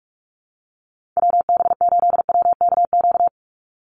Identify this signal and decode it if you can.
Morse code, heard as a keyed tone, about 44 wpm, 700 Hz. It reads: WB8WKQ
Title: Call Signs - 45wpm